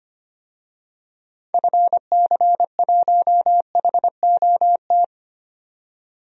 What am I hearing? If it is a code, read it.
Morse code, heard as a keyed tone, about 25 wpm, 700 Hz. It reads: FC1HOT
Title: DXCC-100 Call Signs - 6 Characters - 25wpm